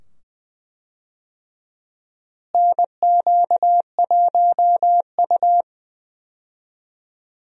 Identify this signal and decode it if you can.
Morse code, heard as a keyed tone, about 20 wpm, 700 Hz. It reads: NQ1U
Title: USA Call Signs - 4 Characters - Repeated 2x - 20wpm